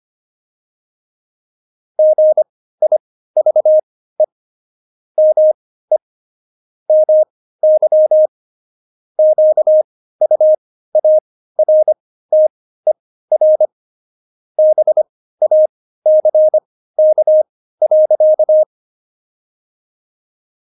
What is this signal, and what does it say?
Morse code, heard as a keyed tone, about 25 wpm, spaced as for 15 wpm, 620 Hz. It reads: GIVE ME MY QUARTER BACK.
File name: Jokes-25-char-15-eff